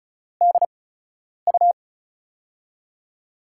Morse code, heard as a keyed tone, about 36 wpm, spaced as for 10 wpm, 700 Hz. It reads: DU
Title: USA Call Signs - 4 Characters - 8x Character Spacing - 35wpm